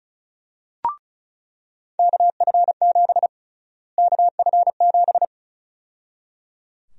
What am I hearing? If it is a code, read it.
Morse code, heard as a keyed tone, about 34 wpm, 700 Hz. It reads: KF7 KF7
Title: USA Call Sign Prefixes - Repeated 2x - 35wpm